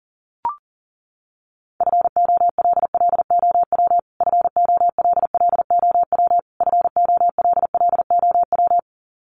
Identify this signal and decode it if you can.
Morse code, heard as a keyed tone, about 40 wpm, 700 Hz. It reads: FOLLOW FOLLOW FOLLOW
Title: Top 300 Words Repeated 40wpm